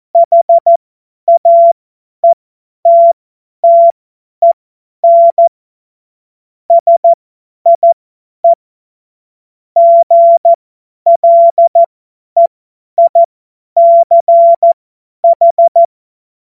Morse code, heard as a keyed tone, about 14 wpm, spaced as for 10 wpm, 680 Hz. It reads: HAETTEN SIE GLEICH
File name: Morsekurs / Witze / Witz 21.November 2023 50 F70